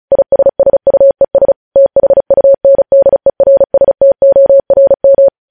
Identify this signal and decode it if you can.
Morse code, heard as a keyed tone, about 36 wpm, 550 Hz. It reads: ISSUES THUNDERSTORM
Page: News Headlines in Morse Code at 35 WPM